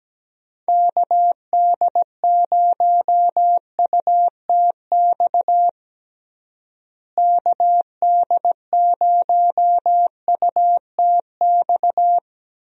Morse code, heard as a keyed tone, about 17 wpm, 700 Hz. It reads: KD0UTX KD0UTX
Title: USA Call Signs - 6 Characters - Repeated 2x - 17wpm